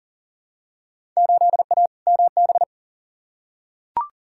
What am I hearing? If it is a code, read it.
Morse code, heard as a keyed tone, about 40 wpm, 700 Hz. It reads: 8A MB
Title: Field Day Exchanges Encore - 40wpm